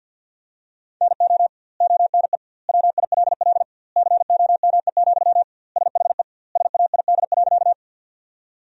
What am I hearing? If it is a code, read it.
Morse code, heard as a keyed tone, about 52 wpm, 700 Hz. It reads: NO ONE WILL COME, SHE SAID.
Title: 2x Word-Spacing - Sentences from Top 100 Words - 50wpm